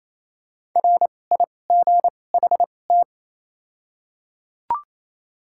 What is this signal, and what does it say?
Morse code, heard as a keyed tone, about 28 wpm, 700 Hz. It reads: R I G H T
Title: 2x Character Spacing - Top 200 Words - 28wpm